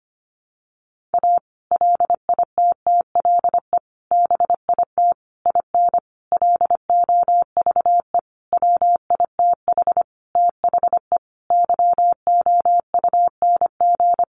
Morse code, heard as a keyed tone, about 25 wpm, 700 Hz. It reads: A LITTLE BIT IN LOVE WITH THE YOUNG
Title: Sentences from Top 900 Words 25wpm